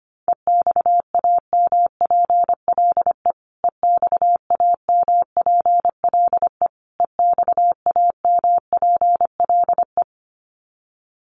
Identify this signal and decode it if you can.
Morse code, heard as a keyed tone, about 25 wpm, 700 Hz. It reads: EXAMPLE EXAMPLE EXAMPLE
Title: Top 500 Words Repeated 25wpm